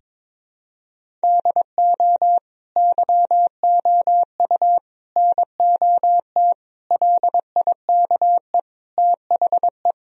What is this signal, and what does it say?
Morse code, heard as a keyed tone, about 21 wpm, 700 Hz. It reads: DO YOU NOT LIKE THE
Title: Sentences from Top 1000 Words - Review Part 1 - 22wpm